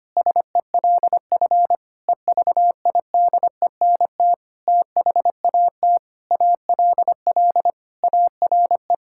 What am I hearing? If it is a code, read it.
Morse code, heard as a keyed tone, about 25 wpm, 700 Hz. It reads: SELF EVIDENT THAT ALL ARE
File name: practice / 25wpm / Sentences with 9-Word Phrases - 25wpm